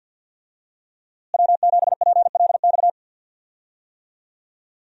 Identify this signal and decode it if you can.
Morse code, heard as a keyed tone, about 52 wpm, 700 Hz. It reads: W7PLX
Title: USA Call Signs - 5 Characters - Repeated 3x - 50wpm